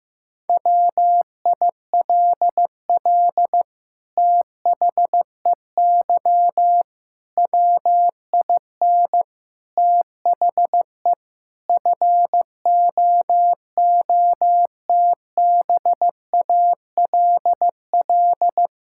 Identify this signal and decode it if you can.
Morse code, heard as a keyed tone, about 15 wpm, 700 Hz. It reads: WILL THEY WIN THE FOOTBALL
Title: Sentences from Top 1600 Words 15wpm